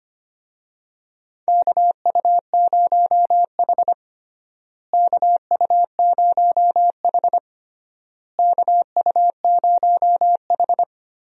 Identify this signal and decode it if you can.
Morse code, heard as a keyed tone, about 25 wpm, 700 Hz. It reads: KU0H KU0H KU0H
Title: USA Call Signs - 4 Characters - Repeated 3x - 25wpm